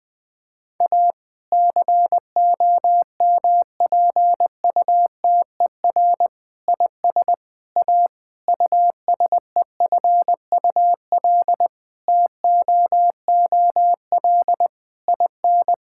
Morse code, heard as a keyed tone, about 21 wpm, 700 Hz. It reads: A COMPUTER IS A USEFUL TOOL IN
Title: Sentences from Top 1000 Words - Review Part 6 - 20wpm